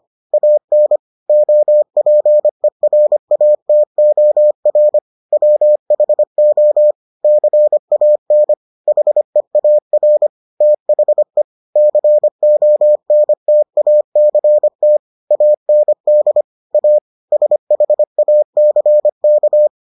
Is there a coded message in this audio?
Yes: Morse code from 0.3 to 19.7 s, heard as a keyed tone, about 25 wpm, 600 Hz. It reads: AN OPERATOR WHO CAN HEAR THE CONTACT AND A SHACK